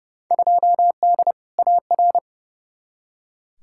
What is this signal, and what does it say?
Morse code, heard as a keyed tone, about 29 wpm, 700 Hz. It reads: 2D AR
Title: Field Day Exchanges Encore - 30wpm